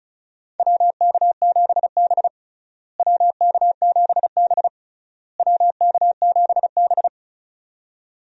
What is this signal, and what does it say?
Morse code, heard as a keyed tone, about 36 wpm, 700 Hz. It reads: WK7B WK7B WK7B
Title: USA Call Signs - 4 Characters - Repeated 3x - 35wpm